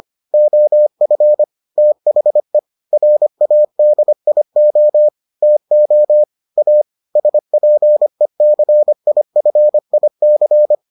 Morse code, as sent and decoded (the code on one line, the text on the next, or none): --- ..-. / - .... / .-. .- -.. .. --- / - --- / .- / ... .--. -.-. .. ..-. .. -.-.
OF THE RADIO TO A SPECIFIC